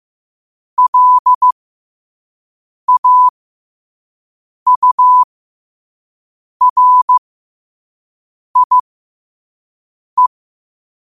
Morse code, heard as a keyed tone, about 15 wpm, spaced as for 5 wpm, 1000 Hz. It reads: LAURIE